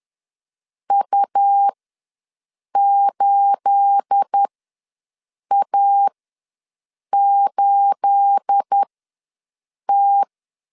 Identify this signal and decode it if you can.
Morse code, heard as a keyed tone, about 10 wpm, spaced as for 6 wpm, 800 Hz. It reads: U8A8T
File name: LEZIONI / 9 286 ws 10wpm 4032